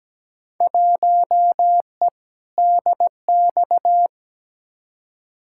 Morse code, heard as a keyed tone, about 17 wpm, 700 Hz. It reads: 1E DX
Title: Field Day Exchanges Encore - 17wpm